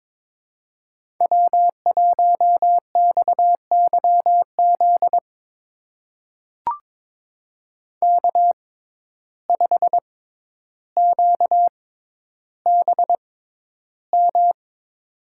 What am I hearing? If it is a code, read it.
Morse code, heard as a keyed tone, about 22 wpm, spaced as for 11 wpm, 700 Hz. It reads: W1XYZ K5QBM
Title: USA Call Signs - 5 Characters - 6x Character Spacing - 22wpm